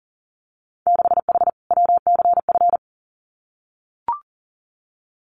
Morse code, heard as a keyed tone, about 38 wpm, 700 Hz. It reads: 6H WCF
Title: Winter Field Day Exchanges Encore 40wpm